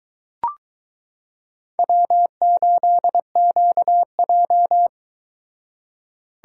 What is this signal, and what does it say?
Morse code, heard as a keyed tone, about 22 wpm, 700 Hz. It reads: W8QJ